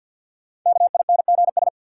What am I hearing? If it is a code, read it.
Morse code, heard as a keyed tone, about 51 wpm, 700 Hz. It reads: KINGS
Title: Sentences from Top 1000 Words - Review Part 6 - New Word Drill - 50wpm